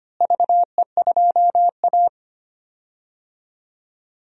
Morse code, heard as a keyed tone, about 25 wpm, 700 Hz. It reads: VE2A
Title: DXCC-100 Call Signs - 4 Characters - Repeated 3x - 25wpm